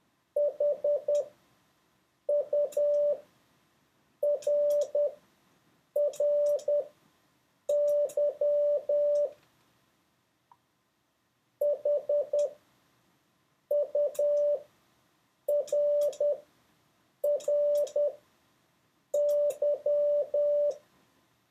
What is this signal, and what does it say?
Morse code, heard as a keyed tone, about 10 wpm, spaced as for 6 wpm, 580 Hz. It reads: HURRY HURRY
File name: 4of6-words-1